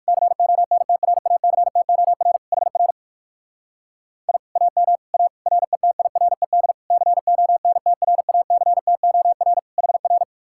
Morse code, heard as a keyed tone, about 59 wpm, 700 Hz. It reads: CONTRACTOR HR I AM A RETIRED CONTRACTOR HR